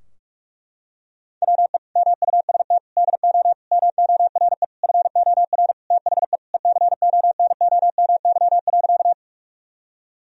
Morse code, heard as a keyed tone, about 41 wpm, 700 Hz. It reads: WE MUST DO MORE FOR THE ECONOMY.
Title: Sentences from Top 1000 Words - Review Part 3 - 45wpm